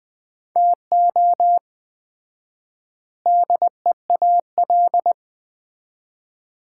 Morse code, heard as a keyed tone, about 20 wpm, 700 Hz. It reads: TO DEAL
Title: 4x Word-Spacing - Sets of 3 Words - 20wpm